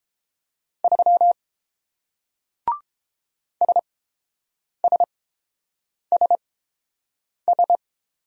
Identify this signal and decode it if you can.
Morse code, heard as a keyed tone, about 30 wpm, spaced as for 7 wpm, 700 Hz. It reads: 3 SSSS